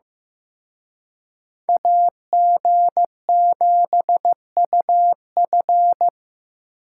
Morse code, heard as a keyed tone, about 18 wpm, 700 Hz. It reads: AG7UF
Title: Speed Racing - USA 2x2 Call Signs 15wpm